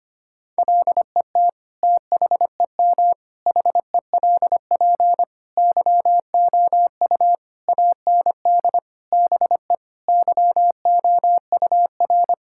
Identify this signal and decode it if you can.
Morse code, heard as a keyed tone, about 25 wpm, 700 Hz. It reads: LET THEM HELP YOU AND BE YOUR
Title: Sentences from Top 1600 Words 25wpm